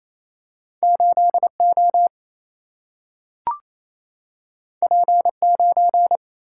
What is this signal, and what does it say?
Morse code, heard as a keyed tone, about 28 wpm, 700 Hz. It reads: 8O P9
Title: Single-Letter Single-Number 28wpm